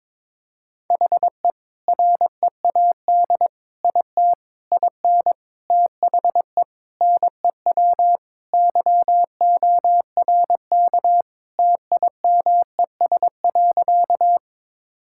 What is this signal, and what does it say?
Morse code, heard as a keyed tone, about 22 wpm, 700 Hz. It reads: HE READ IT IN THE NEW YORK TIMES.
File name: practice / 22wpm / Sentences with 4-Word Phrases - 22wpm